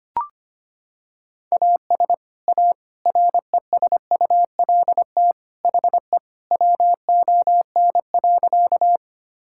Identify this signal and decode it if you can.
Morse code, heard as a keyed tone, about 25 wpm, 700 Hz. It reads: AS A RESULT HE WON.
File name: practice / 25wpm / Sentences with 3-Word Phrases - 25wpm